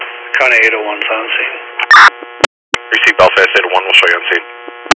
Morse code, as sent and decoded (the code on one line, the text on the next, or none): none